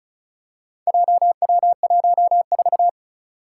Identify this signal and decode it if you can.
Morse code, heard as a keyed tone, about 36 wpm, 700 Hz. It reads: JW14
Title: Four Challenging Letter-Number Combinations 35wpm